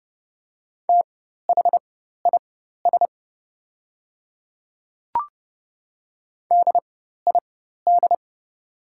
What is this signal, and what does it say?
Morse code, heard as a keyed tone, about 30 wpm, spaced as for 14 wpm, 700 Hz. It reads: THIS DID
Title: 4x Character Spacing - Top 100 Words - 30wpm